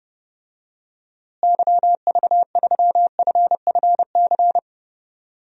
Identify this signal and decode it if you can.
Morse code, heard as a keyed tone, about 32 wpm, 700 Hz. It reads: YV3FFC
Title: DXCC-100 Call Signs - 6 Characters - Repeated 3x - 30wpm